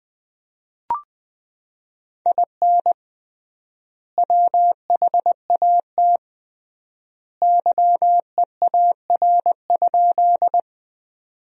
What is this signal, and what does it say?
Morse code, heard as a keyed tone, about 20 wpm, 700 Hz. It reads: IN WHAT YEAR?
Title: 3x Word-Spacing - Sentences from Top 200 Words - 20wpm